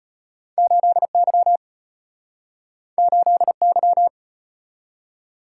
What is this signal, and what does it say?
Morse code, heard as a keyed tone, about 34 wpm, 690 Hz. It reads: 8Y 8Y